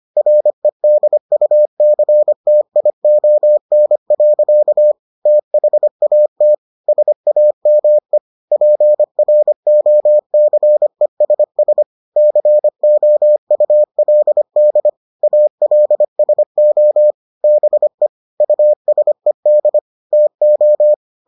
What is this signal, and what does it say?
Morse code, heard as a keyed tone, about 25 wpm, 600 Hz. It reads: REDUCTION. THAT SAME PROCESS COULD ALSO BE USED TO